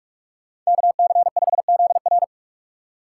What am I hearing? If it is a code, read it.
Morse code, heard as a keyed tone, about 46 wpm, 700 Hz. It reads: KC5ZR
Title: USA Call Signs - 5 Characters - 3x Character Spacing - 45wpm